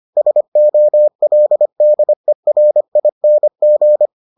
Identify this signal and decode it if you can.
Morse code, heard as a keyed tone, about 25 wpm, 600 Hz. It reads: SOLDERING